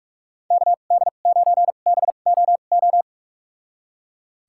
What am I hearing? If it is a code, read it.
Morse code, heard as a keyed tone, about 44 wpm, spaced as for 32 wpm, 700 Hz. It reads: KD9BOO
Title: USA Call Signs - 2x Character Spacing - 45wpm